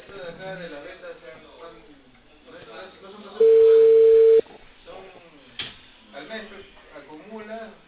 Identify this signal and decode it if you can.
Ringback tone (China, 450 Hz), with speech.